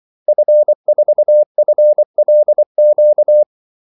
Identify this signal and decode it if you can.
Morse code, heard as a keyed tone, about 24 wpm, 600 Hz. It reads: F4FLQ
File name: Callsigns-24wpm